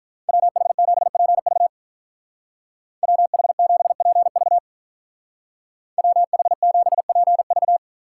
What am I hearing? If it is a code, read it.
Morse code, heard as a keyed tone, about 44 wpm, 700 Hz. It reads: WH7PV WH7PV WH7PV